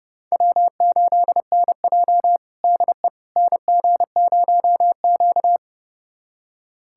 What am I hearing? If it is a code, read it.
Morse code, heard as a keyed tone, about 30 wpm, 700 Hz. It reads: W8NJ DE NG0Q
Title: USA Callsign DE Callsign - 4 Characters - 30wpm